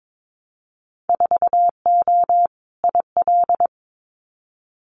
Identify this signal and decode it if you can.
Morse code, heard as a keyed tone, about 22 wpm, 700 Hz. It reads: 4O IL
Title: Winter Field Day Exchanges 22wpm